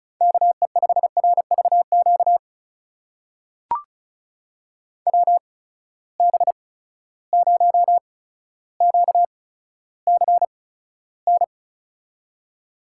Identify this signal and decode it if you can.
Morse code, heard as a keyed tone, about 36 wpm, spaced as for 15 wpm, 700 Hz. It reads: KE5RVQ WB0QCN
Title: USA Call Signs - 6 Characters - 8x Character Spacing - 35wpm